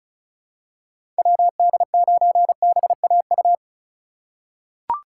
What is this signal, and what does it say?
Morse code, heard as a keyed tone, about 36 wpm, 700 Hz. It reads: WD9BAU
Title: USA Call Signs - 6 Characters - 2x Character Spacing - 35wpm